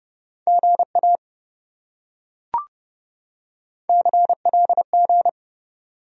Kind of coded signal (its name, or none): Morse code